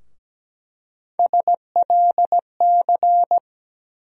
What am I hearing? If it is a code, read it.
Morse code, heard as a keyed tone, about 17 wpm, 700 Hz. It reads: SLC
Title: USA Call Sign Postfixes - 3 Letters 17wpm